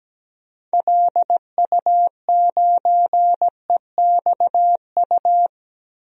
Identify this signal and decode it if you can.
Morse code, heard as a keyed tone, about 17 wpm, 700 Hz. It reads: LU9EXU
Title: DXCC-100 Call Signs - 6 Characters - Repeated 3x - 17wpm